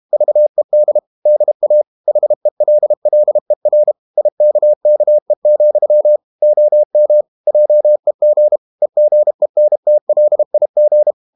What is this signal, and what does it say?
Morse code, heard as a keyed tone, about 32 wpm, 600 Hz. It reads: VED DA HELLER IKKE, OM JEG EGENTLIG
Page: Ferie 32 wpm